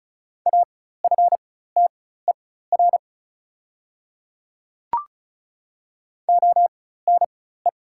Morse code, heard as a keyed tone, about 36 wpm, spaced as for 16 wpm, 700 Hz. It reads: AFTER ONE
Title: 4x Character Spacing - Top 200 Words - 35wpm